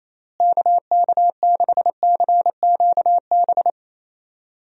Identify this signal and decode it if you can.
Morse code, heard as a keyed tone, about 28 wpm, 700 Hz. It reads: KK6CQB
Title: USA Call Signs - 6 Characters - Repeated 2x - 28wpm